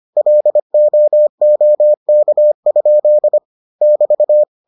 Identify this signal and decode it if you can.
Morse code, heard as a keyed tone, about 25 wpm, 600 Hz. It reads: LOOK? <BT>